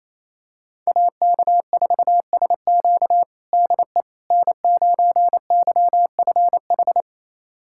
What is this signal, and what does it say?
Morse code, heard as a keyed tone, about 28 wpm, 700 Hz. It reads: AK4SQ DE N9YFH
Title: USA Callsign DE Callsign - 5 Characters - 28wpm